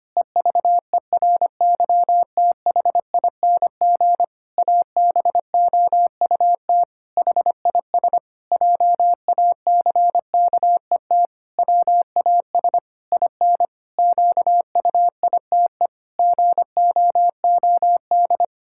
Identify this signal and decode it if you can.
Morse code, heard as a keyed tone, about 25 wpm, 700 Hz. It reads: EVERYTHING ABOUT HIS JACKET WAS IN QUITE GOOD
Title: Pangram Challenge v2 - 25wpm